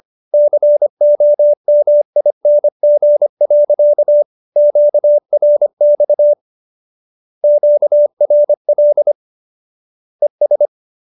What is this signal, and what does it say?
Morse code, heard as a keyed tone, about 25 wpm, 590 Hz. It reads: COMING. QRX QRL ES